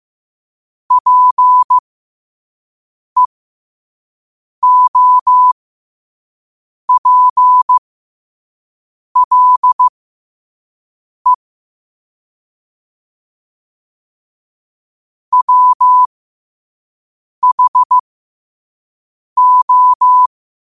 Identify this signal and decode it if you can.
Morse code, heard as a keyed tone, about 15 wpm, spaced as for 5 wpm, 1000 Hz. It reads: PEOPLE WHO
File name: qotdfiles / 5wpm